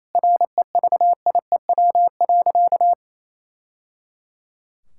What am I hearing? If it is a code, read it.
Morse code, heard as a keyed tone, about 28 wpm, 700 Hz. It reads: REVIEW.
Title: Sentences from Top 1000 Words - Review Part 6 - 28wpm